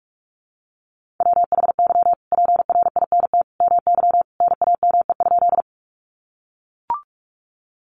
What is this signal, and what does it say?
Morse code, heard as a keyed tone, about 44 wpm, 700 Hz. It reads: WHY PRINT MY NAME?
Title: Sentences from Top 1400 Words 45wpm